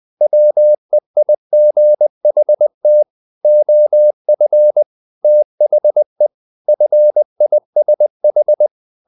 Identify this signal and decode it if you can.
Morse code, heard as a keyed tone, about 20 wpm, 600 Hz. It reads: WEIGHT OF THE FISH
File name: the old man and the sea / the old man and the sea part 1 20wpm